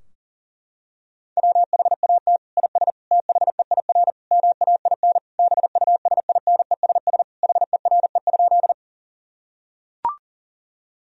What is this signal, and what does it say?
Morse code, heard as a keyed tone, about 37 wpm, 700 Hz. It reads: WHAT IS THEIR MAIN BUSINESS HERE?
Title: Sentences from Top 1000 Words - Review Part 2 - 40wpm